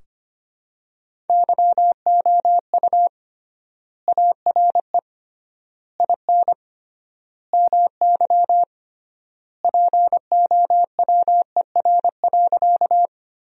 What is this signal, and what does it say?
Morse code, heard as a keyed tone, about 25 wpm, 700 Hz. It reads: YOU ARE IN MY POWER.
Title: 3x Word-Spacing - Sentences from Top 500 Words - 25wpm